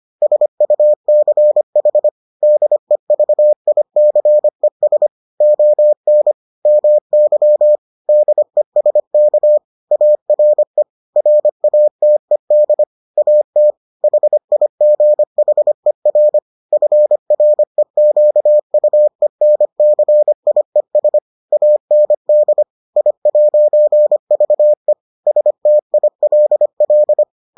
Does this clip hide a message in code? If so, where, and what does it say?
Yes: Morse code from 0.2 to 27.2 s, heard as a keyed tone, about 25 wpm, 600 Hz. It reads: SUCH DEVICES ON MY DESK ARE RATED AT HIGHER FREQUENCIES AND I'VE STILL